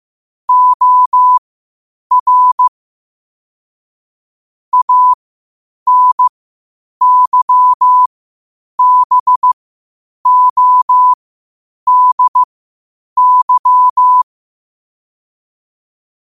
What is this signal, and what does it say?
Morse code, heard as a keyed tone, about 15 wpm, spaced as for 8 wpm, 1000 Hz. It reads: OR ANYBODY
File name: qotdfiles / 8wpm